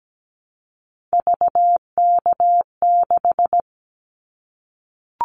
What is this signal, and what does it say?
Morse code, heard as a keyed tone, about 17 wpm, 700 Hz. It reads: VK6
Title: Callsign Prefixes for Top 50 DXCC - 17wpm